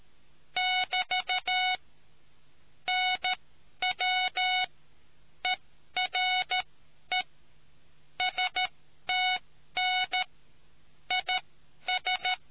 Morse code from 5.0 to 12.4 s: ERE STN IS